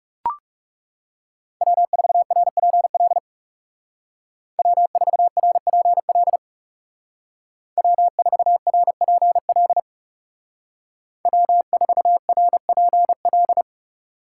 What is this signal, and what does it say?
Morse code, heard as a keyed tone, about 35 wpm, 700 Hz. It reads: W4RPL W4RPL W4RPL W4RPL
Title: Speed Racing - USA 1x3 Call Signs 30wpm